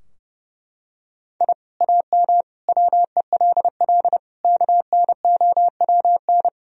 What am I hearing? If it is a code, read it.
Morse code, heard as a keyed tone, about 32 wpm, 700 Hz. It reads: I AM WELL KNOWN
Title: Sentences from Top 1000 Words - Review Part 3 - 30wpm